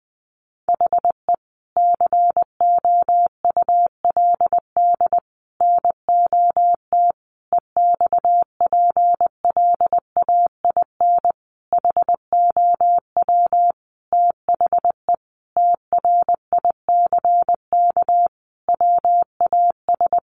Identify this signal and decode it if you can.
Morse code, heard as a keyed tone, about 20 wpm, 700 Hz. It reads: HE COULD NOT EXPLAIN HOW THE TRICK WAS
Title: Sentences from Top 1500 Words 20wpm